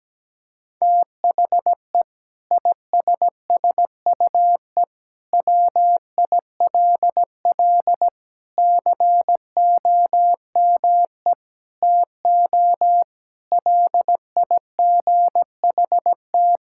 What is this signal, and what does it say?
Morse code, heard as a keyed tone, about 17 wpm, 700 Hz. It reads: THE ISSUE WILL COME TO LIGHT
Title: Sentences from Top 1000 Words - Review Part 6 - 17wpm